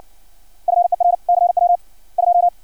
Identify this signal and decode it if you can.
Morse code, heard as a keyed tone, about 56 wpm, 700 Hz. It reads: FER CW 2
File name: archived / CRSnet